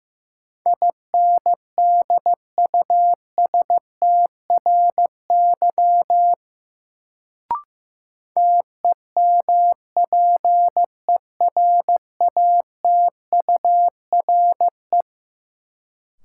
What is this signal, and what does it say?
Morse code, heard as a keyed tone, about 15 wpm, 700 Hz. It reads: INDUSTRY TEMPERATURE
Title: Sentences from Top 1000 Words - Review Part 4 - New Word Drill - 15wpm